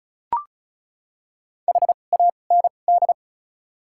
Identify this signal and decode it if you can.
Morse code, heard as a keyed tone, about 34 wpm, spaced as for 25 wpm, 700 Hz. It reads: HAND
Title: 2x Character Spacing - Top 300 Words - 35wpm